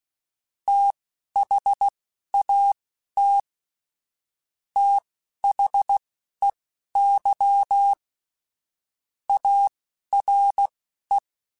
Morse code, heard as a keyed tone, about 16 wpm, spaced as for 11 wpm, 800 Hz. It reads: THAT THEY ARE
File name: Technical Subjects / Technical Subjects-mp3s / Sw Pwr Sup @10